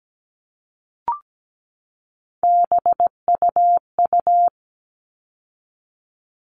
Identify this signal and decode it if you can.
Morse code, heard as a keyed tone, about 17 wpm, 700 Hz. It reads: BUU
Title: All Cut Numbers - 3-Digit 17wpm